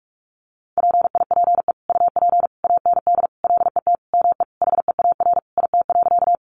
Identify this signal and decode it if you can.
Morse code, heard as a keyed tone, about 44 wpm, 700 Hz. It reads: PIPE UP AND LET ME HEAR IT.
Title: Sentences from Top 1500 Words 45wpm